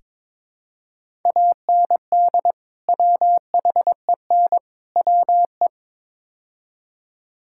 Morse code, heard as a keyed tone, about 22 wpm, 700 Hz. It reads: AND WHEN WE
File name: practice / 22wpm / Build A Phrase - 5-Word Phrases - 22wpm